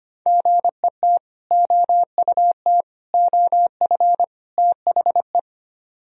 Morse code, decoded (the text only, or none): GET OUT OF THE